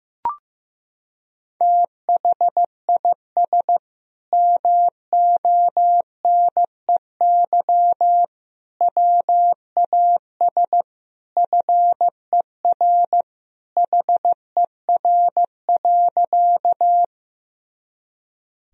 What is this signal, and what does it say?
Morse code, heard as a keyed tone, about 16 wpm, 700 Hz. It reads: THIS MONEY WAS FER HER.
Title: Sentences from Top 400 Words and Abbreviations 15wpm